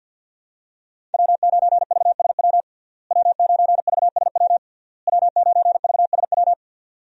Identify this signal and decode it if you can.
Morse code, heard as a keyed tone, about 52 wpm, 700 Hz. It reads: W9VSW W9VSW W9VSW
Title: USA Call Signs - 5 Characters - Repeated 3x - 50wpm